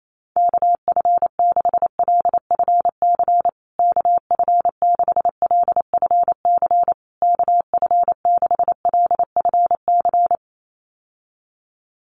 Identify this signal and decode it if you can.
Morse code, heard as a keyed tone, about 28 wpm, 700 Hz. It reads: KF6LFC KF6LFC KF6LFC
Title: USA Call Signs Repeated 3x 28wpm